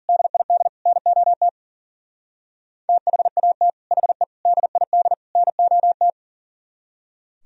Morse code, heard as a keyed tone, about 51 wpm, 700 Hz. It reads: DID NOT THAT HE DID NOT